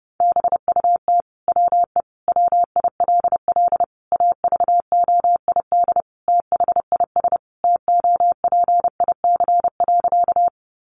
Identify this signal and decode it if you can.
Morse code, heard as a keyed tone, about 30 wpm, 700 Hz. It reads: BUT WE WILL AVOID THIS TOPIC.
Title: Sentences from Top 1200 Words 30wpm